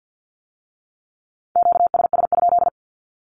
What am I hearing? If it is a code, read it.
Morse code, heard as a keyed tone, about 50 wpm, 700 Hz. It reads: QSS?
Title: Q-Codes with Spoken Meaning - 50wpm